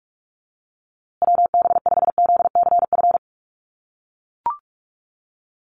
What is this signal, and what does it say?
Morse code, heard as a keyed tone, about 44 wpm, 700 Hz. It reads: WB5ZCF
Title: USA Call Signs Repeated 3x 45wpm